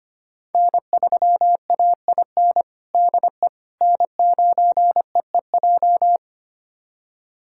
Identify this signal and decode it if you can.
Morse code, heard as a keyed tone, about 25 wpm, 700 Hz. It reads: N3AIN DE N9EEJ
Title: USA Callsign DE Callsign 25wpm